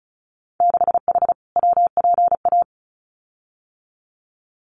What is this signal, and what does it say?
Morse code, heard as a keyed tone, about 34 wpm, 700 Hz. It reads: 6H WPA
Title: Winter Field Day Exchanges Encore 35wpm